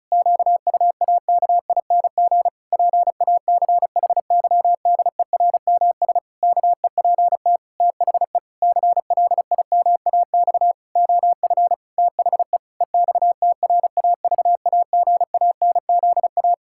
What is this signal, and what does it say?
Morse code, heard as a keyed tone, about 36 wpm, 700 Hz. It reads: QUAKING PACHYDERMS KEPT THE CLIMAX OF THE EXTRAVAGANZA
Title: Pangram Challenge v2 - 35wpm